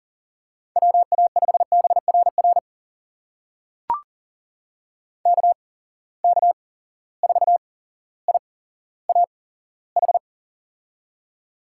Morse code, heard as a keyed tone, about 39 wpm, spaced as for 17 wpm, 700 Hz. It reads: WA5BRR KK4IAH
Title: USA Call Signs - 6 Characters - 8x Character Spacing - 40wpm